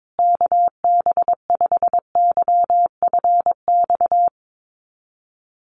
Morse code, heard as a keyed tone, about 22 wpm, 700 Hz. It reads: KB5YFX
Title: USA Call Signs Repeated 3x 22wpm